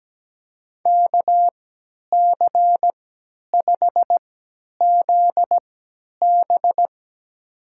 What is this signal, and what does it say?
Morse code, heard as a keyed tone, about 17 wpm, spaced as for 10 wpm, 700 Hz. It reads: KC5ZB
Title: USA Call Signs - 3x Character Spacing - 17wpm